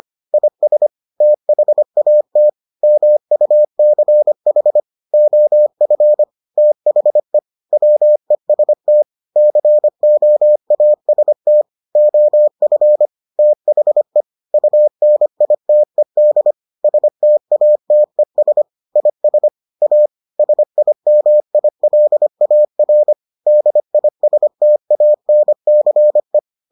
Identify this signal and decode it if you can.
Morse code, heard as a keyed tone, about 25 wpm, 600 Hz. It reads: IS THAT MUCH OF THE WEST COAST OF THE UNITED STATES IS A SIMILAR DISTANCE